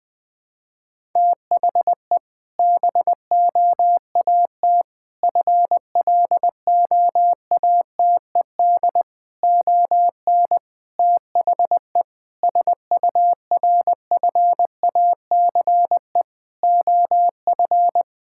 Morse code, heard as a keyed tone, about 20 wpm, 700 Hz. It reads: THE BOAT FLOATED ON THE SURFACE OF
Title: Sentences with 4-Word Phrases - 20wpm